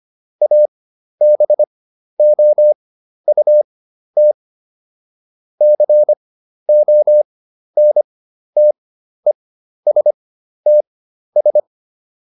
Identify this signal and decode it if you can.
Morse code, heard as a keyed tone, about 25 wpm, spaced as for 12 wpm, 600 Hz. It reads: ABOUT CONTESTS